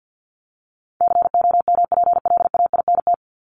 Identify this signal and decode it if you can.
Morse code, heard as a keyed tone, about 50 wpm, 700 Hz. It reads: COMPLAINT
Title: Top 1251-1500 Words 50wpm